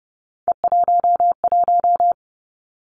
Morse code, heard as a keyed tone, about 32 wpm, 700 Hz. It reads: E11
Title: Three Challenging Letter-Number Combinations 30wpm